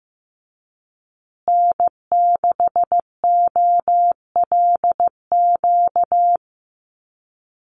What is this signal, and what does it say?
Morse code, heard as a keyed tone, about 15 wpm, 700 Hz. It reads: N6OLQ
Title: USA Call Signs Repeated 3x 15wpm